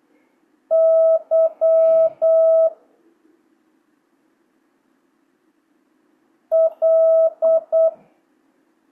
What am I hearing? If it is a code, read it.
Morse code, heard as a keyed tone, about 8 wpm, 650 Hz. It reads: Y L
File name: jh8cbh / 2025cw